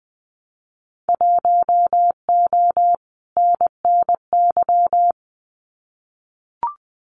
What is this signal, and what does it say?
Morse code, heard as a keyed tone, about 20 wpm, 700 Hz. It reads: 1O NNY
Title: Winter Field Day Exchanges 20wpm